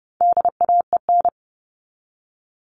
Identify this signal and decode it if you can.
Morse code, heard as a keyed tone, about 30 wpm, 700 Hz. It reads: DAEN
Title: All Cut Numbers - 4-Digit 30wpm